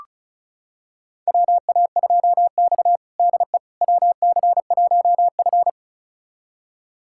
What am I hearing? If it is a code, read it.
Morse code, heard as a keyed tone, about 36 wpm, 700 Hz. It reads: WA2X DE WC1F
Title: USA Callsign DE Callsign - 4 Characters - 35wpm